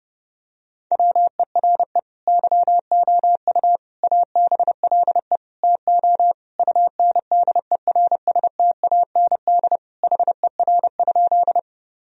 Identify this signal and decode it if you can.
Morse code, heard as a keyed tone, about 30 wpm, 700 Hz. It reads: WERE YOU ABLE TO UNDERSTAND HER?
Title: Sentences from Top 1000 Words - Review Part 2 - 30wpm